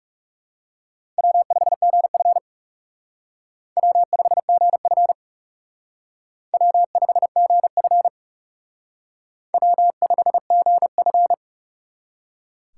Morse code, heard as a keyed tone, about 36 wpm, 700 Hz. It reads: W5GF W5GF W5GF W5GF
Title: Speed Racing - USA 1x2 Call Signs 30wpm